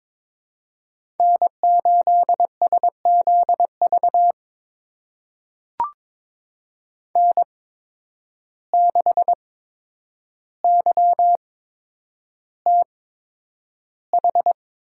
Morse code, heard as a keyed tone, about 22 wpm, spaced as for 9 wpm, 700 Hz. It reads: N8SZV N6YTH
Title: USA Call Signs - 5 Characters - 8x Character Spacing - 22wpm